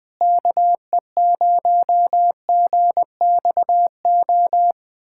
Morse code, heard as a keyed tone, about 20 wpm, 700 Hz. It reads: KE0GXO
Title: USA Callsign DE Callsign - 6 Characters - 20wpm